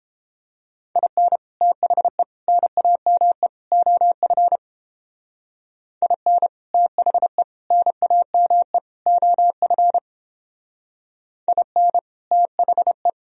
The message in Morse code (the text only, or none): IN THE NAME OF IN THE NAME OF IN THE